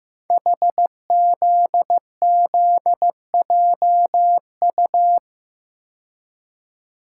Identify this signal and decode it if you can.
Morse code, heard as a keyed tone, about 15 wpm, 700 Hz. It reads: HZZJU
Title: Code Groups - 5 Letters - 15wpm